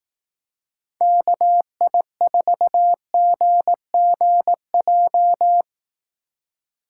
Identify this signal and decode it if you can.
Morse code, heard as a keyed tone, about 18 wpm, 700 Hz. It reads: KI4GGJ
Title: Speed Racing - USA Call Signs 15wpm